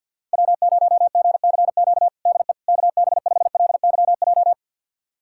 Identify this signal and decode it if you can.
Morse code, heard as a keyed tone, about 47 wpm, 700 Hz. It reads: W0GCQ DE KB5LYJ
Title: USA Callsign DE Callsign 50wpm